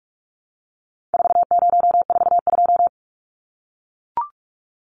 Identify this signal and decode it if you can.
Morse code, heard as a keyed tone, about 44 wpm, 700 Hz. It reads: 4042
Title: Numbers - 4-Digit 45wpm